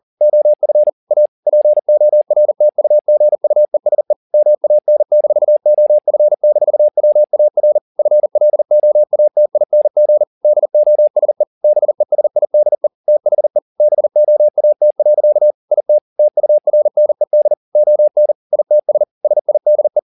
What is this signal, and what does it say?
Morse code, heard as a keyed tone, about 40 wpm, 600 Hz. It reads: OF A PORTUGUESE MAN-OF-WAR FLOATING DOSE BESIDE THE BOAT. IT TURNED ON ITS SIDE